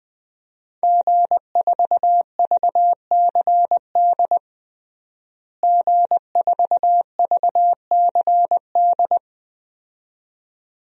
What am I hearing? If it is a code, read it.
Morse code, heard as a keyed tone, about 20 wpm, 700 Hz. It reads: G4VCD G4VCD
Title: DXCC-100 Call Signs - 5 Characters - Repeated 2x - 20wpm